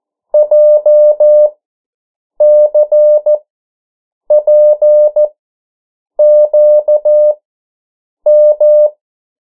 Morse code, heard as a keyed tone, about 14 wpm, spaced as for 7 wpm, 600 Hz. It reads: JCPQM
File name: morse mp3 / 09 urok 8